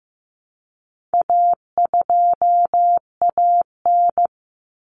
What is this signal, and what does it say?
Morse code, heard as a keyed tone, about 15 wpm, 700 Hz. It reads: A2AN
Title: Common Cut Numbers - 4-Digit 15wpm